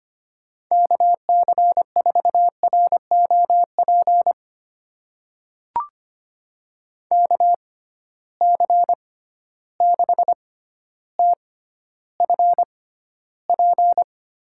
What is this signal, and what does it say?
Morse code, heard as a keyed tone, about 25 wpm, spaced as for 13 wpm, 700 Hz. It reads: KC4ROP KC6TFP
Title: USA Call Signs - 6 Characters - 6x Character Spacing - 25wpm